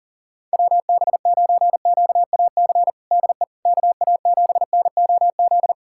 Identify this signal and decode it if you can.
Morse code, heard as a keyed tone, about 40 wpm, 700 Hz. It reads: WB9QAC DE KA7NOZ
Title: USA Callsign DE Callsign - 6 Characters - 40wpm